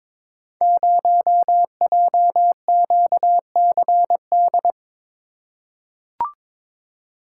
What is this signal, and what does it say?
Morse code, heard as a keyed tone, about 22 wpm, 700 Hz. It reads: 0JQCD
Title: Code Groups - 5 Letter:Numbers - 22wpm